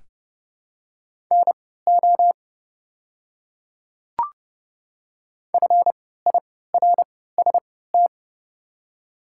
Morse code, heard as a keyed tone, about 30 wpm, spaced as for 17 wpm, 700 Hz. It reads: NO FIRST
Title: 3x Character Spacing - Top 200 Words - 30wpm